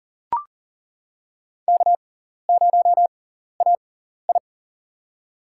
Morse code, heard as a keyed tone, about 40 wpm, spaced as for 14 wpm, 700 Hz. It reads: K0AI
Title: USA Call Signs - 4 Characters - 6x Character Spacing - 40wpm